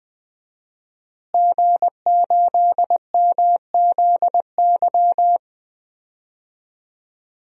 Morse code, heard as a keyed tone, about 20 wpm, 700 Hz. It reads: G8MZY
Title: DXCC-100 Call Signs - 5 Characters - Repeated 2x - 20wpm